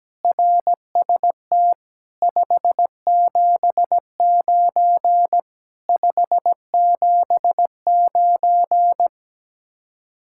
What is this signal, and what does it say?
Morse code, heard as a keyed tone, about 17 wpm, 700 Hz. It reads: RST 579 579
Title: Speed-Racing Common QSO Phrases Encore - 17wpm